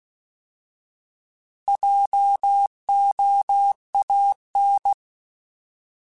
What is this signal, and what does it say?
Morse code, heard as a keyed tone, about 16 wpm, 800 Hz. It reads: JOAN
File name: Alphabetical / Alphabetical mp3s / Js @15